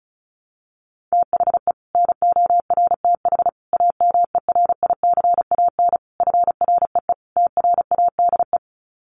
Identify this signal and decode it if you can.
Morse code, heard as a keyed tone, about 34 wpm, 700 Hz. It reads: THE NORTH AMERICAN FREE TRADE
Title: Sets of 5 Words Encore 35wpm